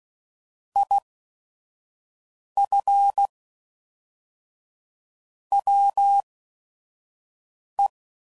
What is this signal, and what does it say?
Morse code, heard as a keyed tone, about 16 wpm, spaced as for 5 wpm, 800 Hz. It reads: IF WE
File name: CW Code Groups @5WPM / QSOs / QSO WA7QOX 1